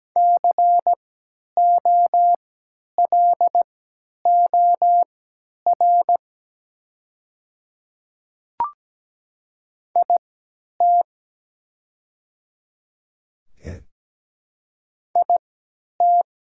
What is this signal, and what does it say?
Morse code, heard as a keyed tone, about 17 wpm, spaced as for 10 wpm, 700 Hz. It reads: COLOR IT IT
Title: 3x Character Spacing - Top 300 Words - 17wpm